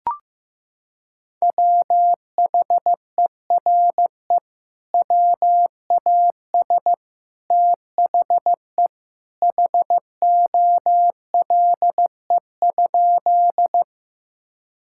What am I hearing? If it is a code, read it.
Morse code, heard as a keyed tone, about 15 wpm, 700 Hz. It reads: WHERE WAS THE HOLE?
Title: Sentences from Top 1600 Words 15wpm